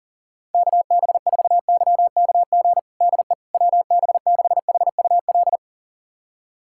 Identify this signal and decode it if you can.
Morse code, heard as a keyed tone, about 38 wpm, 700 Hz. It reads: KB4YKG DE WB6HUL
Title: USA Callsign DE Callsign 40wpm